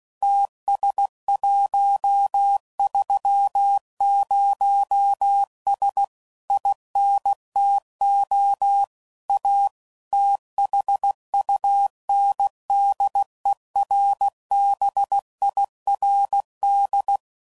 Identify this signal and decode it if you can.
Morse code, heard as a keyed tone, about 16 wpm, 800 Hz. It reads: TS130S INTO A THUNDERBIRD
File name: CW Code Groups @15WPM / QSOs / WD0BGP Spaced